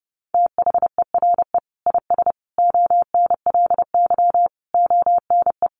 Morse code, heard as a keyed tone, about 30 wpm, 700 Hz. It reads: THERE IS ONLY ONE
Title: Sentences from Top 1200 Words 30wpm